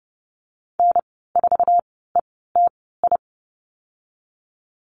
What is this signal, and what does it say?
Morse code, heard as a keyed tone, about 30 wpm, spaced as for 17 wpm, 700 Hz. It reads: N4ETI
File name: Lesson 19g - Letter w - Call Signs 30wpm